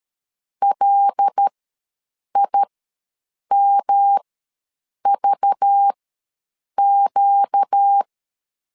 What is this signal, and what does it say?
Morse code, heard as a keyed tone, about 13 wpm, spaced as for 7 wpm, 800 Hz. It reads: LIMVQ
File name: LEZIONI / 6 PKX0 / 6 PK ws 12wpm 3116